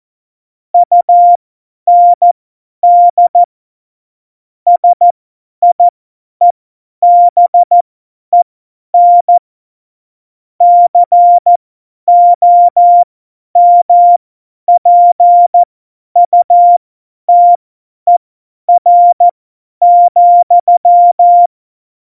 Morse code, heard as a keyed tone, about 14 wpm, spaced as for 10 wpm, 700 Hz. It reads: UND SIEBEN COMPUTER,
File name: Morsekurs / Witze / Witz 23. April 2019, 50 F70